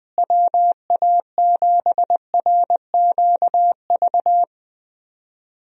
Morse code, heard as a keyed tone, about 20 wpm, 700 Hz. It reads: WA7RQV
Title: USA Call Signs - 6 Characters - Repeated 3x - 20wpm